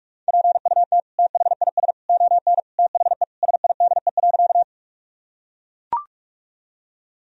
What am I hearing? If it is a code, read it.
Morse code, heard as a keyed tone, about 44 wpm, 700 Hz. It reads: PUT THIS ON THE SIDE.
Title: Sentences from Top 1000 Words - Review Part 1 - 45wpm